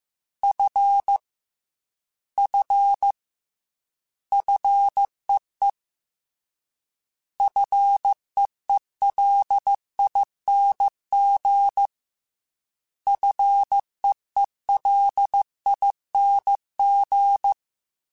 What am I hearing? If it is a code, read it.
Morse code, heard as a keyed tone, about 15 wpm, 800 Hz. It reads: F F FEE FEELING FEELING
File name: Combinations / Combinations-mp3s / Combinations@13